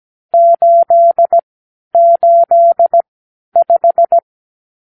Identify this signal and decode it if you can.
Morse code, heard as a keyed tone, about 17 wpm, spaced as for 11 wpm, 680 Hz. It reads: 885
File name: k7qo / Lesson 54 (9)